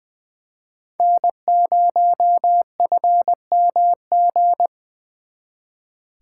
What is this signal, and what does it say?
Morse code, heard as a keyed tone, about 24 wpm, 700 Hz. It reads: N0FMG
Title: Speed Racing - USA 1x3 Call Signs 20wpm